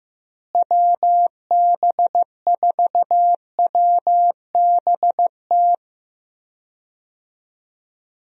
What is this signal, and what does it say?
Morse code, heard as a keyed tone, about 15 wpm, 700 Hz. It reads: WB4WBT
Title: USA Call Signs - 6 Characters - Repeated 3x - 15wpm